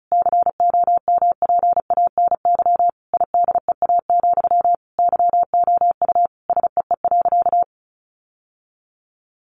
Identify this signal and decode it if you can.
Morse code, heard as a keyed tone, about 34 wpm, 700 Hz. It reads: COMPANY IDEA, YOU SEE.
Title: Sentences from Top 600 Words 35wpm